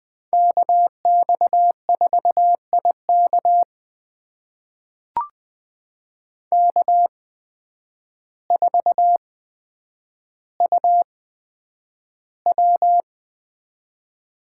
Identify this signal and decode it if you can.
Morse code, heard as a keyed tone, about 20 wpm, spaced as for 9 wpm, 700 Hz. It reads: KX4IK K4UW
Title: USA Call Signs - 5 Characters - 8x Character Spacing - 20wpm